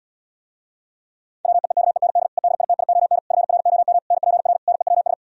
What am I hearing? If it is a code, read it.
Morse code, heard as a keyed tone, about 77 wpm, 700 Hz. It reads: WEEKEND EDITION FROM NPR NEWS